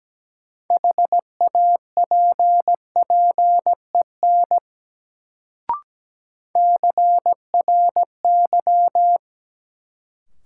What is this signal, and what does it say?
Morse code, heard as a keyed tone, about 17 wpm, 700 Hz. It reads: HAPPEN CRY
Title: Sets of 5 Words Encore - New Word Drill - 17wpm